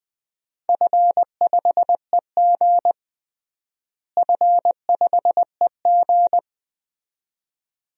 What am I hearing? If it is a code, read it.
Morse code, heard as a keyed tone, about 20 wpm, 700 Hz. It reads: F5EG F5EG
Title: DXCC-100 Call Signs - 4 Characters - Repeated 2x - 20wpm